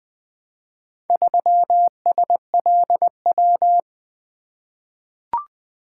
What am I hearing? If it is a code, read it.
Morse code, heard as a keyed tone, about 20 wpm, 700 Hz. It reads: 3SLW
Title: Four Letter-Number Combinations - 20wpm